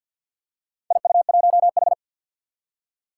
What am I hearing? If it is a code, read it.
Morse code, heard as a keyed tone, about 53 wpm, 700 Hz. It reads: IU1H
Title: Four Challenging Letter-Number Combinations 50wpm